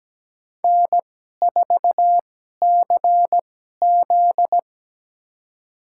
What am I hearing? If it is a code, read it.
Morse code, heard as a keyed tone, about 17 wpm, spaced as for 12 wpm, 700 Hz. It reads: N4CZ
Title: USA Call Signs - 4 Characters - 2x Character Spacing - 17wpm